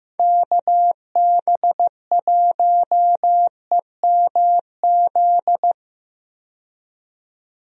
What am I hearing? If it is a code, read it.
Morse code, heard as a keyed tone, about 15 wpm, 700 Hz. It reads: KB1EMZ
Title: USA Call Signs - 6 Characters - 15wpm